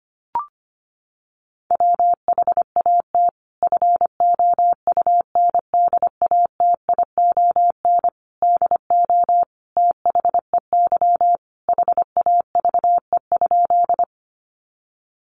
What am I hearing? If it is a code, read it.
Morse code, heard as a keyed tone, about 25 wpm, 700 Hz. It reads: WHAT FOUNDATION DO THEY HAVE?
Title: Sentences from Top 1200 Words 25wpm